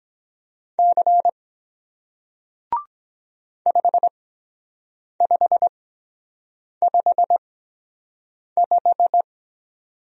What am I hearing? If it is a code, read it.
Morse code, heard as a keyed tone, about 23 wpm, spaced as for 7 wpm, 700 Hz. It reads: C 5555